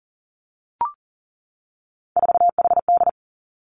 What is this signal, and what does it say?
Morse code, heard as a keyed tone, about 40 wpm, 700 Hz. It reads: 4HD
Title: Three Challenging Letter-Number Combinations 40wpm